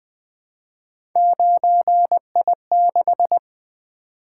Morse code, heard as a keyed tone, about 20 wpm, 700 Hz. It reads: 9I6
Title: Three Letter-Number Combinations - 20wpm